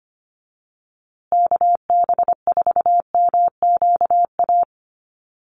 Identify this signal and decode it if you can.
Morse code, heard as a keyed tone, about 25 wpm, 700 Hz. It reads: KB4MQA
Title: USA Call Signs Repeated 3x 25wpm